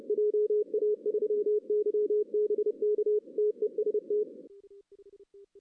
Morse code, heard as a keyed tone, about 28 wpm, 380 Hz. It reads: JA3YBK TEST
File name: 2016 arrl dx cw / 2016-ARRL-DX-CW-K5TR- - 0-3